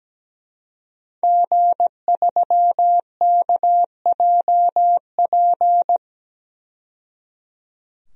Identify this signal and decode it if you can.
Morse code, heard as a keyed tone, about 17 wpm, 700 Hz. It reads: G3KJP
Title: DXCC-100 Call Signs - 5 Characters - 17wpm